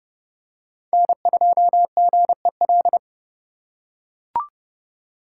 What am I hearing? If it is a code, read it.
Morse code, heard as a keyed tone, about 28 wpm, 700 Hz. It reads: N2GEL